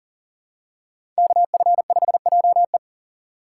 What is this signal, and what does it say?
Morse code, heard as a keyed tone, about 40 wpm, 700 Hz. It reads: KF5JE
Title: USA Call Signs - 5 Characters - 2x Character Spacing - 40wpm